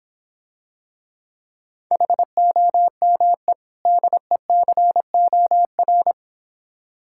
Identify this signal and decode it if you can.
Morse code, heard as a keyed tone, about 26 wpm, 700 Hz. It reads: HOME DECOR